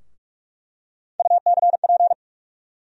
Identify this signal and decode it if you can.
Morse code, heard as a keyed tone, about 45 wpm, 700 Hz. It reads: UCP
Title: USA Call Sign Postfixes - 3 Letters 45wpm